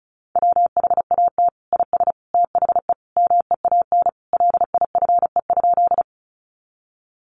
Morse code, heard as a keyed tone, about 34 wpm, 700 Hz. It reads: WHAT IS THE MEAN LIFE?
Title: Sentences from Top 900 Words 35wpm